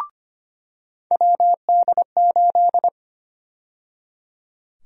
Morse code, heard as a keyed tone, about 25 wpm, 700 Hz. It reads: WD8
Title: USA Call Sign Prefixes - 25wpm